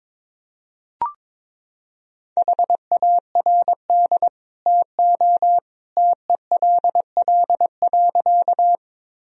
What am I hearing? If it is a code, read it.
Morse code, heard as a keyed tone, about 22 wpm, 700 Hz. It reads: HARD TO TELL.
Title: Sentences from Top 1000 Words - Review Part 1 - 22wpm